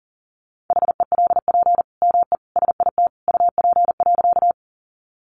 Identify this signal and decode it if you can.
Morse code, heard as a keyed tone, about 40 wpm, 700 Hz. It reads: HELP ME SIT UP.
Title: Sentences from Top 500 Words 40wpm